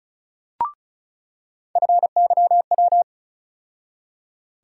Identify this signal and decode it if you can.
Morse code, heard as a keyed tone, about 36 wpm, 700 Hz. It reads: FYW
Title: USA Call Sign Postfixes - 3 Letters 35wpm